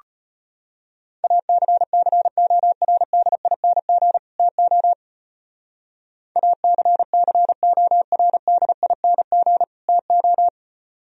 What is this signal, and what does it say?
Morse code, heard as a keyed tone, about 31 wpm, 700 Hz. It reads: ACCORDING TO ACCORDING TO
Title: Speed-Racing - Sets of 2 Words 25wpm